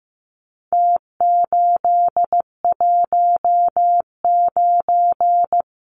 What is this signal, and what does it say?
Morse code, heard as a keyed tone, about 15 wpm, 700 Hz. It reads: T819
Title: Numbers - Contest Style - 4-Digit 15wpm